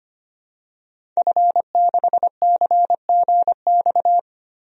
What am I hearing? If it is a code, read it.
Morse code, heard as a keyed tone, about 25 wpm, 700 Hz. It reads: F6CGX
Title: DXCC-100 Call Signs - 5 Characters - Repeated 3x - 25wpm